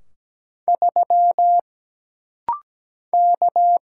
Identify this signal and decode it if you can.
Morse code, heard as a keyed tone, about 17 wpm, 700 Hz. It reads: 3 K
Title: Single Letter-Number - Rapid-Fire 17wpm